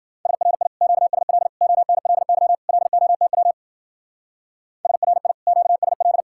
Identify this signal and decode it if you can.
Morse code, heard as a keyed tone, about 50 wpm, 700 Hz. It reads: SRI QSL ONLY LOTW SRI QSL
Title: Speed-Racing Common QSO Phrases - 40wpm